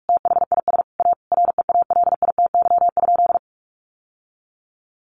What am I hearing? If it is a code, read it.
Morse code, heard as a keyed tone, about 44 wpm, 700 Hz. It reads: THIS A REALITY?
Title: Sentences from Top 1200 Words 45wpm